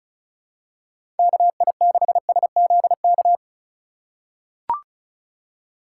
Morse code, heard as a keyed tone, about 34 wpm, 700 Hz. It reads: KI6SZK
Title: USA Call Signs - 6 Characters - 3x Character Spacing - 35wpm